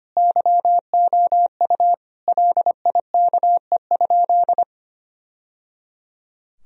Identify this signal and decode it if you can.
Morse code, heard as a keyed tone, about 25 wpm, 700 Hz. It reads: YOU LIKE?
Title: Sentences from Top 1000 Words - Review Part 3 - 25wpm